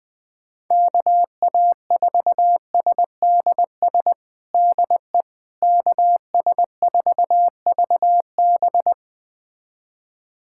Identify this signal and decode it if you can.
Morse code, heard as a keyed tone, about 20 wpm, 700 Hz. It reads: KA4SDS DE KS4VB
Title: USA Callsign DE Callsign 20wpm